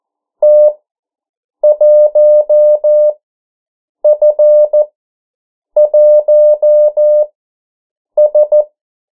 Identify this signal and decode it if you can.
Morse code, heard as a keyed tone, about 14 wpm, spaced as for 7 wpm, 600 Hz. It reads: T1F1S
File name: morse mp3 / 09 urok 8